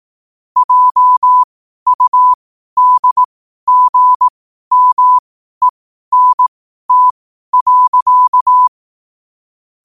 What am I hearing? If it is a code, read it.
Morse code, heard as a keyed tone, about 18 wpm, spaced as for 12 wpm, 1000 Hz. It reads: JUDGMENT.
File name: qotdfiles / 12wpm